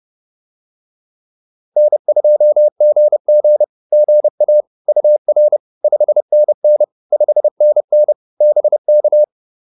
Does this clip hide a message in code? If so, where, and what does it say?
Yes: Morse code from 1.8 to 9.2 s, heard as a keyed tone, about 30 wpm, 590 Hz. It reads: N2GG GA UR 5NN 5NN BK